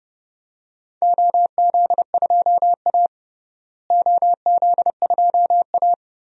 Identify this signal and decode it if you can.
Morse code, heard as a keyed tone, about 30 wpm, 700 Hz. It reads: OZ2A OZ2A
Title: DXCC-100 Call Signs - 4 Characters - Repeated 3x - 30wpm